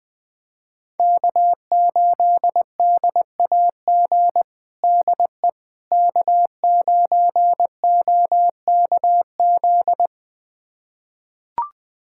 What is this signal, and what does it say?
Morse code, heard as a keyed tone, about 20 wpm, 700 Hz. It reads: K8DAG DE K9OKZ
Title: USA Callsign DE Callsign - 5 Characters - 20wpm